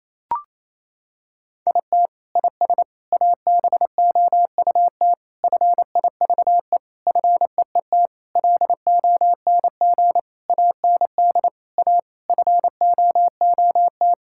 Morse code, heard as a keyed tone, about 28 wpm, 700 Hz. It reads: IT IS ABOUT FIVE FEET LONG AND A FOOT
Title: Sentences from Top 1000 Words - Review Part 6 - 28wpm